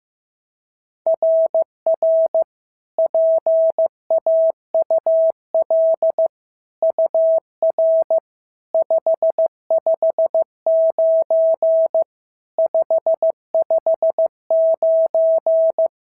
Morse code, heard as a keyed tone, about 14 wpm, 650 Hz. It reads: RR PAUL UR 559 559
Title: SOTA QSO Practice 15wpm